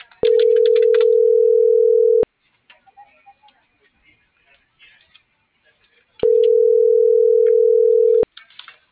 Ringback tone (North America), 440 + 480 Hz.